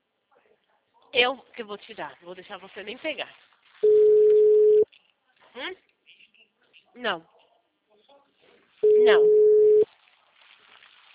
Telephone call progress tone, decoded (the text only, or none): ringback tone